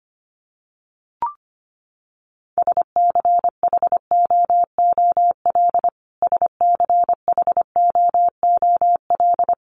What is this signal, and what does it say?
Morse code, heard as a keyed tone, about 25 wpm, 700 Hz. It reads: SCHOOL SCHOOL
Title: Top 500 Words Repeated 25wpm